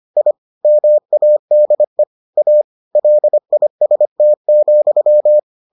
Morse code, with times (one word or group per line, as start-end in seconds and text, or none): 0.2-0.3 I
0.6-2.0 MADE
2.4-2.6 A
3.0-5.4 LIST,